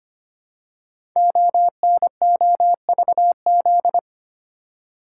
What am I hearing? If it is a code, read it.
Morse code, heard as a keyed tone, about 25 wpm, 700 Hz. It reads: ONOVZ
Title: Code Groups - 5 Letters - 25wpm